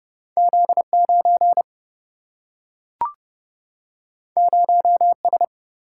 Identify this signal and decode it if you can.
Morse code, heard as a keyed tone, about 30 wpm, 700 Hz. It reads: Z9 0S
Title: Single-Letter Single-Number 30wpm